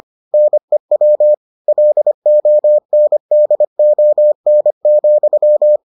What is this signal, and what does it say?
Morse code, heard as a keyed tone, about 25 wpm, 600 Hz. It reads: NEW LONDON,